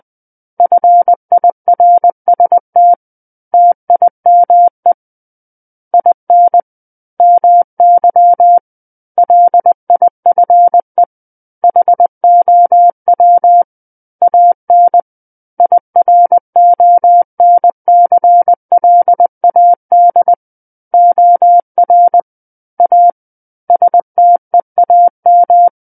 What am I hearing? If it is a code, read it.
Morse code, heard as a keyed tone, about 20 wpm, 700 Hz. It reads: FIRST TIME IN MY LIFE HOW AN IRONCLAD OR A STEAM
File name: 20 WPM